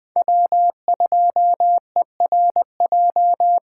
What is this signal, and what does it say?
Morse code, heard as a keyed tone, about 20 wpm, 700 Hz. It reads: W2ERJ